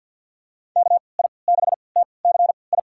Morse code, heard as a keyed tone, about 50 wpm, spaced as for 28 wpm, 700 Hz. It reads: KI6TCI
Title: USA Call Signs - 6 Characters - 3x Character Spacing - 50wpm